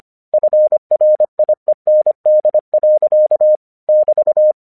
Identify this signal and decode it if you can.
Morse code, heard as a keyed tone, about 26 wpm, 610 Hz. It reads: FRIEND. <BT>
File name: 1 English 25wpm / AQC12